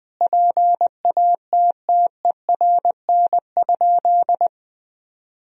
Morse code, heard as a keyed tone, about 20 wpm, 700 Hz. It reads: PATTERN?
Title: Sentences from Top 1000 Words - Review Part 2 - 20wpm